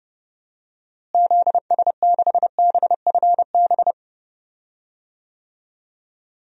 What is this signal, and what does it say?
Morse code, heard as a keyed tone, about 30 wpm, 700 Hz. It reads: ZS6BFB
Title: DXCC-100 Call Signs - 6 Characters - Repeated 2x - 30wpm